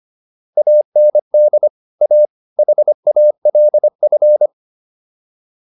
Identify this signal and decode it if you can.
Morse code, heard as a keyed tone, about 25 wpm, 600 Hz. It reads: AND A HALF